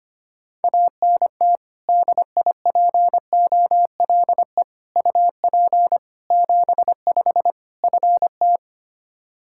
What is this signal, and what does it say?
Morse code, heard as a keyed tone, about 25 wpm, 700 Hz. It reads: ANT DIPOLE UP 75 FT
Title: Practice QSO - Exchange 2 - Info - 25wpm